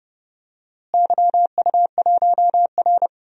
Morse code, heard as a keyed tone, about 30 wpm, 700 Hz. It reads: YU1R
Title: DXCC-100 Call Signs - 4 Characters - Repeated 2x - 30wpm